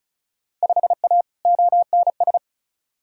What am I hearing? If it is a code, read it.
Morse code, heard as a keyed tone, about 34 wpm, 700 Hz. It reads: 5A ONS
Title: Field Day Exchanges Encore - 35wpm